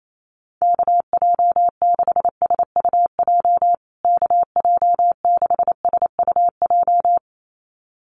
Morse code, heard as a keyed tone, about 28 wpm, 700 Hz. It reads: KJ6SUJ KJ6SUJ
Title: USA Call Signs Repeated 2x 28wpm